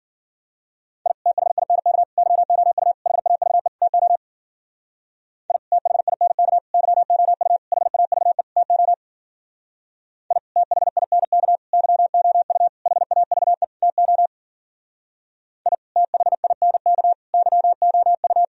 Morse code, heard as a keyed tone, about 52 wpm, 700 Hz. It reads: I THINK YOU HAVE TO I THINK YOU HAVE TO I THINK YOU HAVE TO I THINK YOU